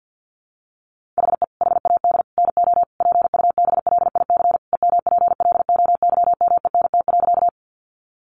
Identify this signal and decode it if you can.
Morse code, heard as a keyed tone, about 50 wpm, 700 Hz. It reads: HE HAD NO PUBLIC EMPLOYMENT.
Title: Sentences from Top 1500 Words 50wpm